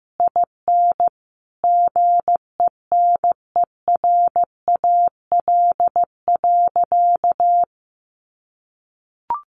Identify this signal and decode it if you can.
Morse code, heard as a keyed tone, about 15 wpm, 700 Hz. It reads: IN GENERAL.
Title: Sentences from Top 1500 Words 15wpm